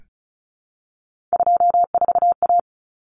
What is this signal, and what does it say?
Morse code, heard as a keyed tone, about 34 wpm, 700 Hz. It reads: 24A
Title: Common Cut Numbers - 3-Digit 35wpm